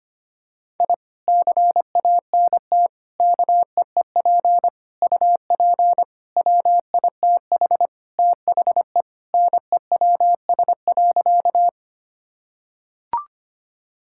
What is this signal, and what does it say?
Morse code, heard as a keyed tone, about 25 wpm, 700 Hz. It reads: I CANT KEEP UP WITH THE NEWS.
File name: practice / 25wpm / Sentences with 2-Word Phrases - 25wpm